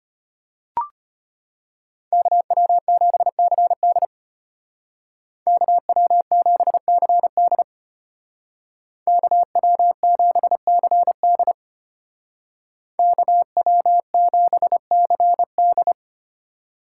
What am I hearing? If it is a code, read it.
Morse code, heard as a keyed tone, about 33 wpm, 700 Hz. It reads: KW7CD KW7CD KW7CD KW7CD